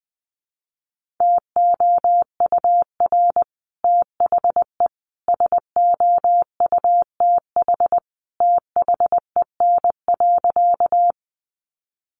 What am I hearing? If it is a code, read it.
Morse code, heard as a keyed tone, about 20 wpm, 700 Hz. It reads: TOUR THE SOUTH THEN.
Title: Sentences from Top 1400 Words 20wpm